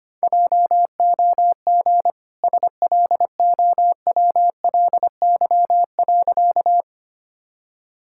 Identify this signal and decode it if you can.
Morse code, heard as a keyed tone, about 25 wpm, 690 Hz. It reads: JOG SLOWLY.
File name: Easy Sentences with Hints - 25wpm